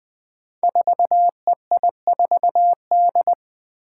Morse code, heard as a keyed tone, about 20 wpm, 700 Hz. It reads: 4EI4D
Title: Five Challenging Letter-Number Combinations 20wpm